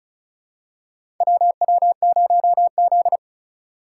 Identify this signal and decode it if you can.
Morse code, heard as a keyed tone, about 36 wpm, 700 Hz. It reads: WW0Z
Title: USA Call Signs - 4x Character Spacing - 35wpm